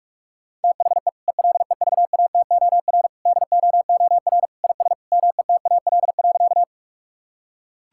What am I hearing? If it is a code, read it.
Morse code, heard as a keyed tone, about 41 wpm, 700 Hz. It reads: THE ELEVATOR DOOR IS METAL.
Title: Sentences from Top 1600 Words 45wpm